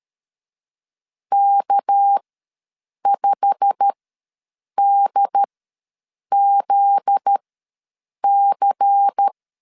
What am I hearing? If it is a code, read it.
Morse code, heard as a keyed tone, about 13 wpm, spaced as for 7 wpm, 800 Hz. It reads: K5DZC
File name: LEZIONI / 8 YJ45 / 8 YJ45 ws 12wpm 3146